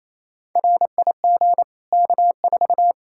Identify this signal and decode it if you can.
Morse code, heard as a keyed tone, about 28 wpm, 700 Hz. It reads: RIG K4
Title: Practice QSO - Exchange 2 - Info - 28wpm